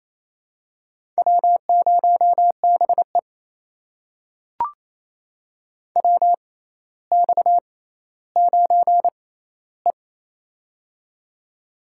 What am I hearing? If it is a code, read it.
Morse code, heard as a keyed tone, about 28 wpm, spaced as for 14 wpm, 700 Hz. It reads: W0BE WX9E
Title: USA Call Signs - 4 Characters - 6x Character Spacing - 28wpm